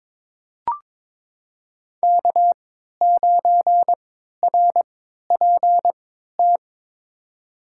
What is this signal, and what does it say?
Morse code, heard as a keyed tone, about 22 wpm, spaced as for 13 wpm, 700 Hz. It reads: K9RPT
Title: USA Call Signs - 5 Characters - 3x Character Spacing - 22wpm